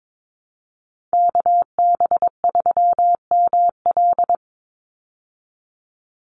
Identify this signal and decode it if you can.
Morse code, heard as a keyed tone, about 22 wpm, 700 Hz. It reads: KB3ML
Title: Call Signs Repeated 3x - 22wpm